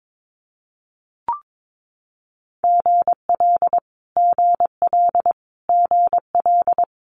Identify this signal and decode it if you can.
Morse code, heard as a keyed tone, about 22 wpm, 700 Hz. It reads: GL GL GL
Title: Multi-letter Phonemes Repeated 3x 22wpm